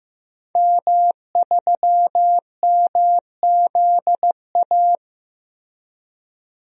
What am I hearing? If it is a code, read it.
Morse code, heard as a keyed tone, about 16 wpm, 700 Hz. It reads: M3MZA
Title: DXCC-100 Call Signs - 5 Characters - 15wpm